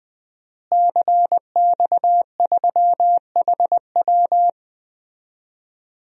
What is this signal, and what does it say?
Morse code, heard as a keyed tone, about 20 wpm, 700 Hz. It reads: CX3HW
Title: DXCC-100 Call Signs - 5 Characters - Repeated 3x - 20wpm